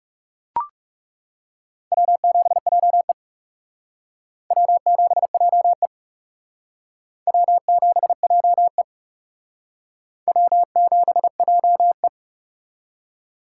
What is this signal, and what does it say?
Morse code, heard as a keyed tone, about 35 wpm, 700 Hz. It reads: W7JE W7JE W7JE W7JE
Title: Speed Racing - USA 1x2 Call Signs 30wpm